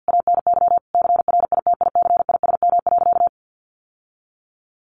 Morse code, heard as a keyed tone, about 49 wpm, 700 Hz. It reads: ANY CRITICISM.
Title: Sentences from Top 1300 Words 50wpm